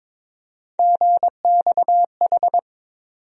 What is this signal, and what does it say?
Morse code, heard as a keyed tone, about 22 wpm, 700 Hz. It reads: GXH
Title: USA Call Sign Postfixes - 3 Letters 22wpm